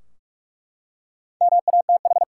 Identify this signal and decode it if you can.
Morse code, heard as a keyed tone, about 46 wpm, 700 Hz. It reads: MATH
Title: Four Letter Words - 45wpm